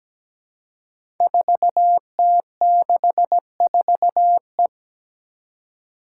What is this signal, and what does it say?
Morse code, heard as a keyed tone, about 17 wpm, 700 Hz. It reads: 4T64E
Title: Five Challenging Letter-Number Combinations 17wpm